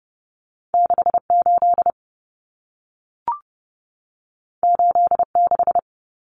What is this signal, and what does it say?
Morse code, heard as a keyed tone, about 30 wpm, 700 Hz. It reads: 68 86
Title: Common Cut Numbers - 2-Digit 30wpm